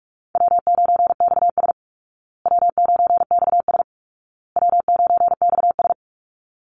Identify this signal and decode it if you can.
Morse code, heard as a keyed tone, about 44 wpm, 700 Hz. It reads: W9XS W9XS W9XS
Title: Call Signs Repeated 2x - 45wpm